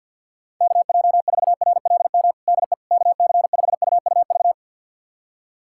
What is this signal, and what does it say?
Morse code, heard as a keyed tone, about 46 wpm, 700 Hz. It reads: KJ4RLM DE KC5UUV
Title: USA Callsign DE Callsign - 6 Characters - 50wpm